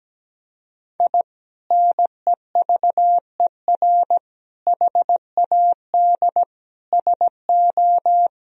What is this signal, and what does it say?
Morse code, heard as a keyed tone, about 17 wpm, 700 Hz. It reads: I NEVER HAD SO
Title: Sentences from Top 1000 Words - Review Part 4 - 17wpm